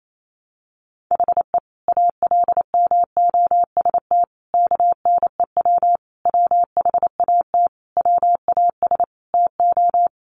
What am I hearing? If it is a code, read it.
Morse code, heard as a keyed tone, about 28 wpm, 700 Hz. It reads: HE ALMOST KNEW WHAT WAS TO
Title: Sentences from Top 700 Words 28wpm